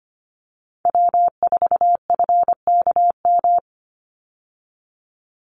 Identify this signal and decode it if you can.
Morse code, heard as a keyed tone, about 25 wpm, 700 Hz. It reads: W4FKM
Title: Call Signs Repeated 3x - 25wpm